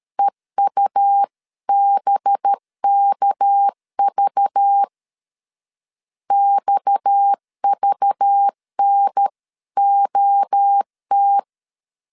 Morse code, heard as a keyed tone, about 13 wpm, 800 Hz. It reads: EUBKV XVNOT
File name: CW / DET / 1 6 1500C S B 12wpm 3431